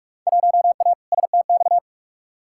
Morse code, heard as a keyed tone, about 45 wpm, 700 Hz. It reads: 1A STX
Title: Field Day Exchanges Encore - 45wpm